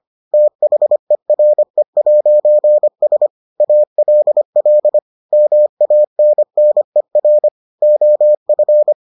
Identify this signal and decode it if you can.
Morse code, heard as a keyed tone, about 25 wpm, 600 Hz. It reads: THERE'S ALL MANNER OF